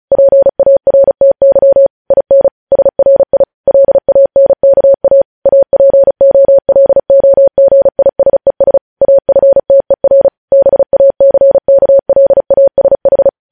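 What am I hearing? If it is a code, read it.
Morse code, heard as a keyed tone, about 36 wpm, 550 Hz. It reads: PARTY IN SRI LANKA APOLOGISES AFTER BACKLASH
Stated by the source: Morse news headlines practice at 35 words per minute.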